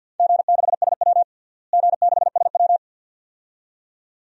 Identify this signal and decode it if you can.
Morse code, heard as a keyed tone, about 52 wpm, 700 Hz. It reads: G6SW G6SW
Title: DXCC-100 Call Signs - 4 Characters - Repeated 2x - 50wpm